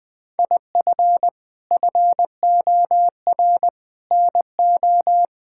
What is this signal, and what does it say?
Morse code, heard as a keyed tone, about 20 wpm, 700 Hz. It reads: IF FOR NO
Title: Sentences with 6-Word Phrases - Spoken Beginning - 20wpm